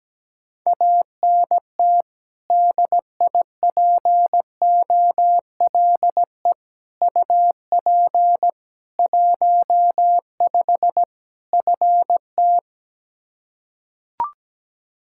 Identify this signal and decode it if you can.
Morse code, heard as a keyed tone, about 17 wpm, 700 Hz. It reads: ANT DIPOLE UP 15 FT
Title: Practice QSO - Exchange 2 - Info - 17wpm